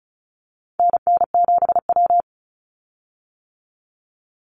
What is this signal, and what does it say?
Morse code, heard as a keyed tone, about 34 wpm, 700 Hz. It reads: NN7W
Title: Call Signs - 35wpm